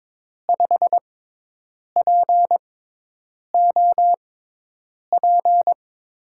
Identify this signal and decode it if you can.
Morse code, heard as a keyed tone, about 22 wpm, spaced as for 8 wpm, 700 Hz. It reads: 5POP
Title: USA Call Signs - 5 Characters - 6x Character Spacing - 22wpm